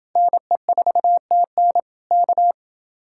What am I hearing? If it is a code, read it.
Morse code, heard as a keyed tone, about 30 wpm, 700 Hz. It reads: NE4TN K